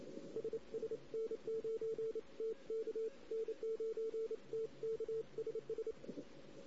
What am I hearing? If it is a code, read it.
Morse code, heard as a keyed tone, about 29 wpm, 450 Hz. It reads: SS N9TK N9TK SS
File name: CW / 2008-ss-cw-k5tr-op-n5rz -16:00